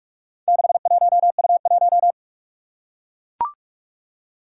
Morse code, heard as a keyed tone, about 44 wpm, 700 Hz. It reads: 61U1
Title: Four Challenging Letter-Number Combinations 45wpm